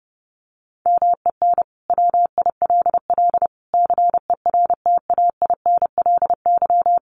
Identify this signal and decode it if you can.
Morse code, heard as a keyed tone, about 30 wpm, 700 Hz. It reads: MEN WILL CERTAINLY
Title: Sentences from Top 1500 Words 30wpm